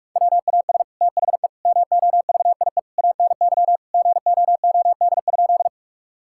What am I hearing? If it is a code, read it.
Morse code, heard as a keyed tone, about 44 wpm, 700 Hz. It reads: WAS THE MOVIE ANY GOOD?
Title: Sentences from Top 1000 Words - Review Part 5 - 45wpm